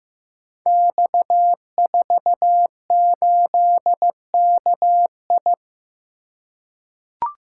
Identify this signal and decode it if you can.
Morse code, heard as a keyed tone, about 15 wpm, 700 Hz. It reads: X48KI
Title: Code Groups - 5 Letter:Numbers - 15wpm